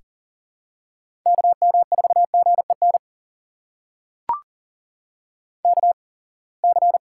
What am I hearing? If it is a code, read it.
Morse code, heard as a keyed tone, about 40 wpm, spaced as for 28 wpm, 700 Hz. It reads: KM4GEN KC
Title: USA Call Signs - 6 Characters - 8x Character Spacing - 40wpm